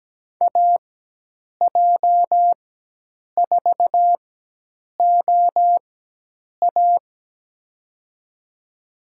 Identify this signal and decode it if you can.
Morse code, heard as a keyed tone, about 17 wpm, spaced as for 8 wpm, 700 Hz. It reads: AJ4OA
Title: USA Call Signs - 4x Character Spacing - 17wpm